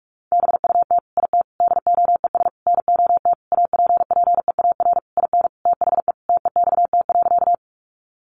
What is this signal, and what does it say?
Morse code, heard as a keyed tone, about 44 wpm, 700 Hz. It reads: BUT IT DOES NOT APPEAR IN THE TEXT.
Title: Sentences from Top 1200 Words 45wpm